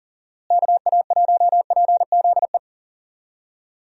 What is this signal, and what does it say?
Morse code, heard as a keyed tone, about 37 wpm, 700 Hz. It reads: KA1PZE
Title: USA Call Signs - 6 Characters - Repeated 3x - 40wpm